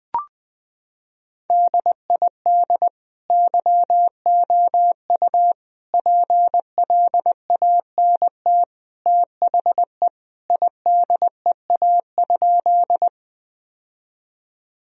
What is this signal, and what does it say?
Morse code, heard as a keyed tone, about 20 wpm, 700 Hz. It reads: DID YOU PLANT THE IDEA?
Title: Sentences from Top 1000 Words - Review Part 1 - 20wpm